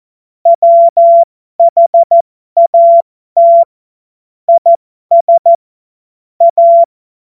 Morse code, heard as a keyed tone, about 14 wpm, spaced as for 12 wpm, 680 Hz. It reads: WHAT IS A
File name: Morsekurs / Witze